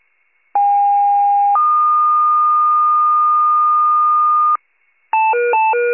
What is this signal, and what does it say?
Two-tone page, heard as a one-second tone, then a three-second tone, about 810 then 1220 Hz. It is heard from 0.5 to 4.6 s.
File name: wcalert